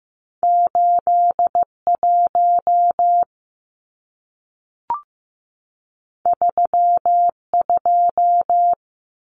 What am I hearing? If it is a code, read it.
Morse code, heard as a keyed tone, about 15 wpm, 700 Hz. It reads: 81 32
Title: Numbers - 2-Digit 15wpm